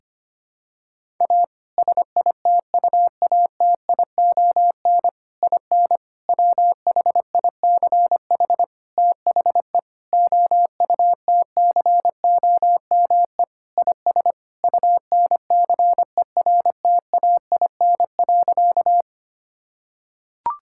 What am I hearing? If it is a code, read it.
Morse code, heard as a keyed tone, about 25 wpm, 700 Hz. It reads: A SITUATION IN WHICH THE OUTCOME IS UNCERTAIN.
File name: practice / 25wpm / Sentences with 3-Word Phrases - 25wpm